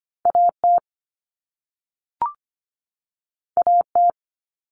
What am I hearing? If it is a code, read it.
Morse code, heard as a keyed tone, about 25 wpm, 700 Hz. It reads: AT AT
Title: Common Cut Numbers - 2-Digit 25wpm